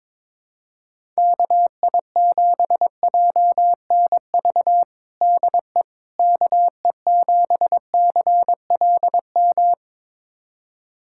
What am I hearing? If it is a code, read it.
Morse code, heard as a keyed tone, about 22 wpm, 700 Hz. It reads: KI7JNV DE KE7CLM
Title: USA Callsign DE Callsign - 6 Characters - 22wpm